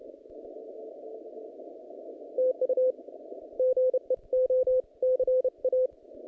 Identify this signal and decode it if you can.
Morse code, heard as a keyed tone, about 31 wpm, 510 Hz. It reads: TU GE O C A
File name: contests / naqp-cw-k5tr-op-k5pi- -0100